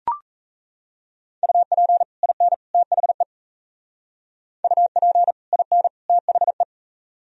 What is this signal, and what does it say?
Morse code, heard as a keyed tone, about 36 wpm, 700 Hz. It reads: UP IN THE UP IN THE